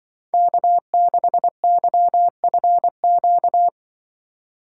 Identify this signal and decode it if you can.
Morse code, heard as a keyed tone, about 24 wpm, 700 Hz. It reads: K6YFQ
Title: Speed Racing - USA 1x3 Call Signs 20wpm